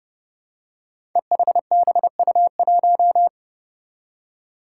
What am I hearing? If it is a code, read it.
Morse code, heard as a keyed tone, about 30 wpm, 700 Hz. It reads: EHBU1
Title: Five Challenging Letter-Number Combinations 30wpm